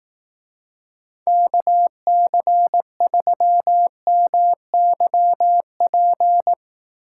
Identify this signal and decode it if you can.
Morse code, heard as a keyed tone, about 18 wpm, 700 Hz. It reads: KC3MYP
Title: Speed Racing - USA 2x3 Call Signs 15wpm